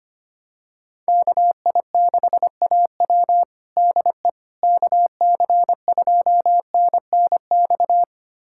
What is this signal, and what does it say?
Morse code, heard as a keyed tone, about 25 wpm, 700 Hz. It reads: KI6AW DE KC2NNX
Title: USA Callsign DE Callsign 25wpm